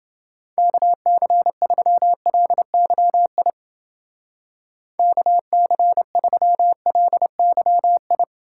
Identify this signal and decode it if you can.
Morse code, heard as a keyed tone, about 29 wpm, 700 Hz. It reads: KC3LYI KC3LYI